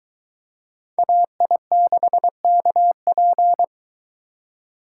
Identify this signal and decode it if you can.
Morse code, heard as a keyed tone, about 23 wpm, 700 Hz. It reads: AI6KP